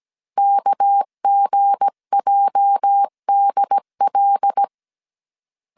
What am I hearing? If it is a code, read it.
Morse code, heard as a keyed tone, about 17 wpm, 800 Hz. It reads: KGJDL
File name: CW / DET / 1 9 1500C S B 16wpm 2856